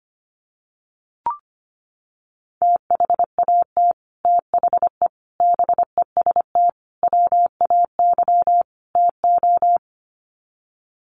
Morse code, heard as a keyed tone, about 25 wpm, 700 Hz. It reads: THAT THE BEST WAY TO
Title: Sets of 5 Words Encore2 25wpm